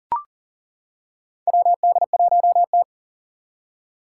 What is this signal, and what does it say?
Morse code, heard as a keyed tone, about 40 wpm, 700 Hz. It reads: WD1T
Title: Four Challenging Letter-Number Combinations 40wpm